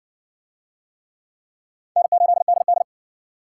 Morse code, heard as a keyed tone, about 54 wpm, 700 Hz. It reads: N8DD